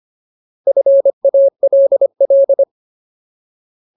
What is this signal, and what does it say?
Morse code, heard as a keyed tone, about 25 wpm, 550 Hz. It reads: FALL
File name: Ultimate Word List / 25WPM Ultimate